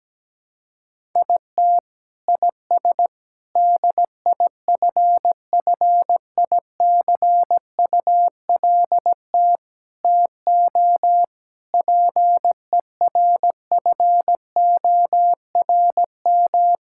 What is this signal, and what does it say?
Morse code, heard as a keyed tone, about 17 wpm, 700 Hz. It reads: IT IS DIFFICULT TO PERFORM
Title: Sentences from Top 1000 Words - Review Part 6 - 17wpm